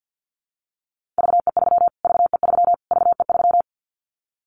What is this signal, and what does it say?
Morse code, heard as a keyed tone, about 49 wpm, 700 Hz. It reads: VE3 VE3 VE3
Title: Callsign Prefixes for Top 25 DXCC - Repeated 3x - 50wpm